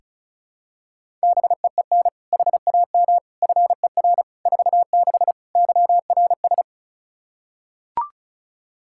Morse code, heard as a keyed tone, about 36 wpm, 700 Hz. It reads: BEEN HAM FER 46 YRS
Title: K1USN SST QSO Practice 35wpm